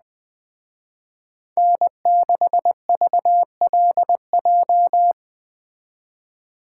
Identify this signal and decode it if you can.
Morse code, heard as a keyed tone, about 20 wpm, 700 Hz. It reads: N6VLJ
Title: Speed Racing - USA 1x3 Call Signs 17wpm